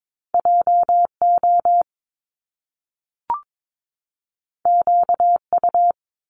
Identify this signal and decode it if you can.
Morse code, heard as a keyed tone, about 22 wpm, 700 Hz. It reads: JO QU
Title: Top Two Letter Combinations in Words 22wpm